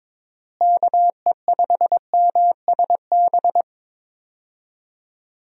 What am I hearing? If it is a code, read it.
Morse code, heard as a keyed tone, about 22 wpm, 700 Hz. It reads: KE5MSB
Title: USA Call Signs - 6 Characters - Repeated 3x - 22wpm